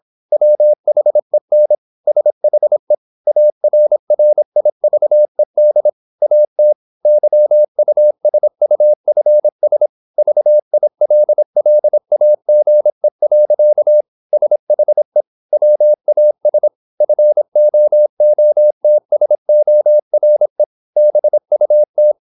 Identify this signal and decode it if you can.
Morse code, heard as a keyed tone, about 26 wpm, 600 Hz. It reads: WHEN SHE ARRIVED AT YUSUFS VILLAGE. SHE WAS FOOTSORE BUT
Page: Never 26wpm | CW med Gnister